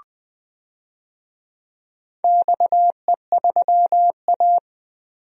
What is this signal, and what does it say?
Morse code, heard as a keyed tone, about 21 wpm, 700 Hz. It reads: XE3A
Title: DXCC-100 Call Signs - 4 Characters - Repeated 2x - 20wpm